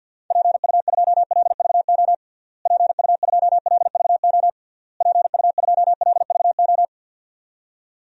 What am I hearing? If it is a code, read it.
Morse code, heard as a keyed tone, about 50 wpm, 700 Hz. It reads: PU2LVO PU2LVO PU2LVO
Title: DXCC-100 Call Signs - 6 Characters - Repeated 3x - 50wpm